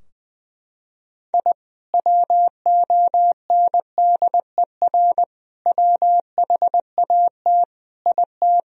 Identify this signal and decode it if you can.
Morse code, heard as a keyed tone, about 20 wpm, 700 Hz. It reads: I WONDER WHAT IT
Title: Sentences from Top 1000 Words - Review Part 3 - 20wpm